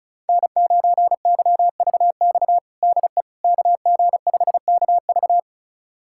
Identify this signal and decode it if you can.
Morse code, heard as a keyed tone, about 36 wpm, 690 Hz. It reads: N9YVX DE KG5KV
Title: USA Callsign DE Callsign - 5 Characters - 35wpm